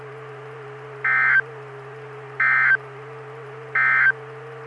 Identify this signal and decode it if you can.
Emergency Alert System (SAME) tones, heard as harsh buzzing bursts. It is heard from 0.9 to 4.2 s.
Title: Saline, AK SKYWARN Repeater System - W5RHS